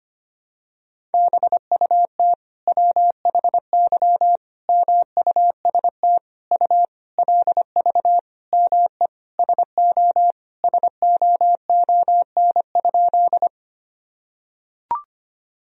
Morse code, heard as a keyed tone, about 25 wpm, 700 Hz. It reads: BUT WHY MUST U LV ME SO SOON?
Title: Sentences from Top 400 Words and Abbreviations 25wpm